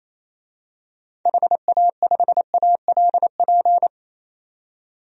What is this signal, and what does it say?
Morse code, heard as a keyed tone, about 27 wpm, 700 Hz. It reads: HA5ALP
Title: DXCC-100 Call Signs - 6 Characters - 28wpm